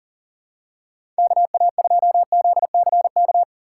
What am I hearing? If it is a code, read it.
Morse code, heard as a keyed tone, about 40 wpm, 700 Hz. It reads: KA2ZCK
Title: USA Call Signs - 6 Characters - 6x Character Spacing - 40wpm